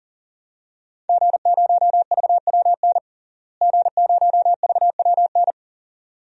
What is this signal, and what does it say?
Morse code, heard as a keyed tone, about 40 wpm, 700 Hz. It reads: G0VWN G0VWN
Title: DXCC-100 Call Signs - 5 Characters - Repeated 2x - 40wpm